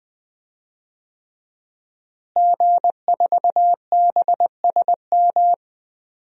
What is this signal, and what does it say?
Morse code, heard as a keyed tone, about 20 wpm, 700 Hz. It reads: G4BSM
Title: DXCC-100 Call Signs - 5 Characters - Repeated 3x - 20wpm